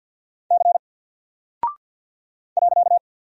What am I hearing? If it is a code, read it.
Morse code, heard as a keyed tone, about 49 wpm, 700 Hz. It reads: C .
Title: Single Character (Letter:Number:Punctuation:Prosign) Alt 50wpm